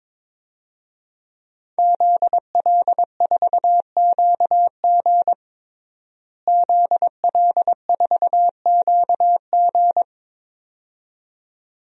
Morse code, heard as a keyed tone, about 22 wpm, 700 Hz. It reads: ZL4QG ZL4QG
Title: DXCC-100 Call Signs - 5 Characters - Repeated 2x - 22wpm